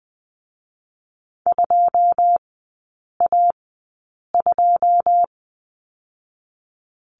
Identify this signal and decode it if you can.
Morse code, heard as a keyed tone, about 20 wpm, 700 Hz. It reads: 2 A 2
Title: 107 Lesson 18e - Letter m - Three Characters 20wpm